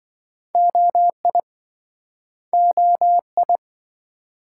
Morse code, heard as a keyed tone, about 27 wpm, 700 Hz. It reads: OI OI